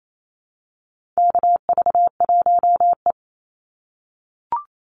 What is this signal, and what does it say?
Morse code, heard as a keyed tone, about 28 wpm, 700 Hz. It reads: KV1E
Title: Call Signs - 28wpm